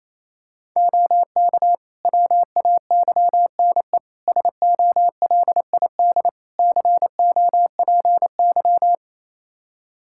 Morse code, heard as a keyed tone, about 27 wpm, 700 Hz. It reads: OK WAYNE SOLID COPY
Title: Practice QSO - Exchange 3 - More Info - 28wpm